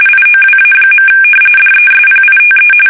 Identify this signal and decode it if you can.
RTTY, about 45.45 baud. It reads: IOWA. DESPITE
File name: fsk850 / battleships